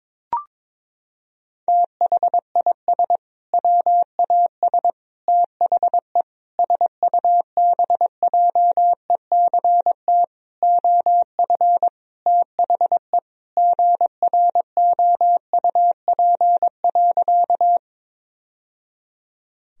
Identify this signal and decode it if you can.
Morse code, heard as a keyed tone, about 22 wpm, 700 Hz. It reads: THIS WAS THE SUBJECT OF THE GROUP.
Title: Sentences from Top 1000 Words - Review Part 1 - 22wpm